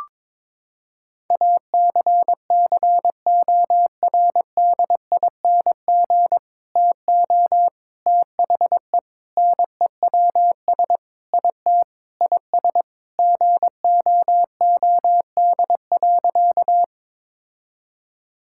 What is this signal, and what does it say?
Morse code, heard as a keyed tone, about 21 wpm, 700 Hz. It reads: ACCORDING TO THE NEWS IT IS GOOD.
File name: practice / 22wpm / Sentences with 3-Word Phrases - 22wpm